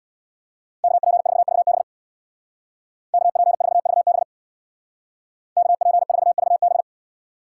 Morse code, heard as a keyed tone, about 61 wpm, 700 Hz. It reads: KP4VB KP4VB KP4VB
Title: Speed Racing - USA Call Signs 50wpm